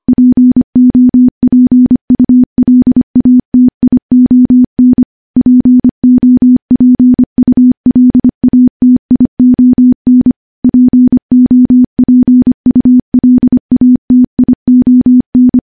Morse code, heard as a keyed tone, about 25 wpm, 250 Hz. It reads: POPULATION POPULATION POPULATION
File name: morse / morse-training-25WPM